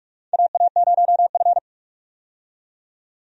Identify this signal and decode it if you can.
Morse code, heard as a keyed tone, about 42 wpm, 700 Hz. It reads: AA0F
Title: USA Call Signs - 4 Characters - 45wpm